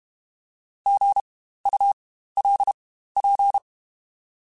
Morse code, heard as a keyed tone, about 32 wpm, spaced as for 15 wpm, 800 Hz. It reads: GULP
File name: Spaced Alphas / Spaced Alphas mp3s / Gs sp @30